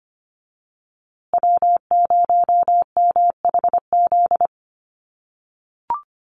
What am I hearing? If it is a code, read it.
Morse code, heard as a keyed tone, about 25 wpm, 700 Hz. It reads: W0MHZ
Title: USA Call Signs Repeated 3x 25wpm